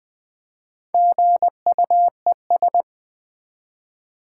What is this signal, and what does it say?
Morse code, heard as a keyed tone, about 20 wpm, 700 Hz. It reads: GUES
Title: Build A Word - 1 Letter At A Time - 5-Letter Words - 20wpm